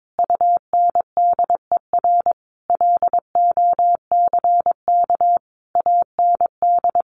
Morse code, heard as a keyed tone, about 22 wpm, 700 Hz. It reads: UNDER LOCK AND
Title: Sentences from Top 1400 Words 22wpm